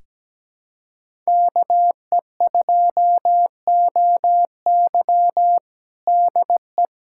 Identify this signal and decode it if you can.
Morse code, heard as a keyed tone, about 17 wpm, 700 Hz. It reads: KE2OY DE
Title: USA Callsign DE Callsign 17wpm